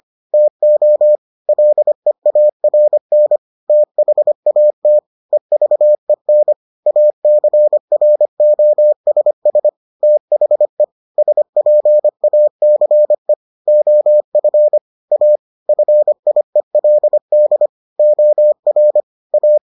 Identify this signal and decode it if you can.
Morse code, heard as a keyed tone, about 25 wpm, 600 Hz. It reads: TO LEARN THAT EVEN ACROSS THE SPACE OF A FIELD OR A